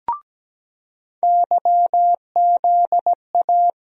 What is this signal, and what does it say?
Morse code, heard as a keyed tone, about 17 wpm, 700 Hz. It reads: YZA
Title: USA Call Sign Postfixes - 3 Letters 17wpm